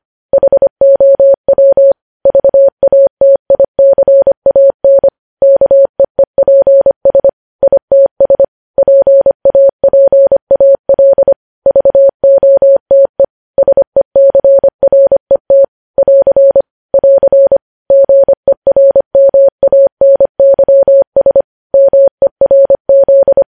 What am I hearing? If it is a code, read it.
Morse code, heard as a keyed tone, about 25 wpm, 550 Hz. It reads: HOW VATICAN KEEPS ITS PAPAL VOTE SECRET <AR> <AR> GERMANYS MERZ
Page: news headlines in morse code at 25 wpm